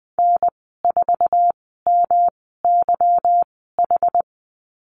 Morse code, heard as a keyed tone, about 20 wpm, spaced as for 15 wpm, 700 Hz. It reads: N4MYH
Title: 167 Lesson 27g - Number 7 - Call Signs 20wpm